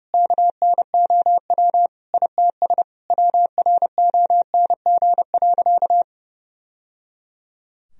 Morse code, heard as a keyed tone, about 30 wpm, 700 Hz. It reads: KNOW ITS WRONG.
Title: Sentences from Top 1000 Words - Review Part 1 - 30wpm